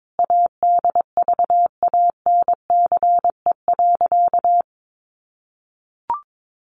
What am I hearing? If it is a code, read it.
Morse code, heard as a keyed tone, about 22 wpm, 700 Hz. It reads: ADVANCE.
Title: Sentences from Top 1400 Words 22wpm